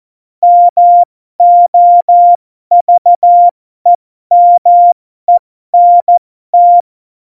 Morse code, heard as a keyed tone, about 14 wpm, spaced as for 12 wpm, 700 Hz. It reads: MOVEMENT
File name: Morsekurs / Witze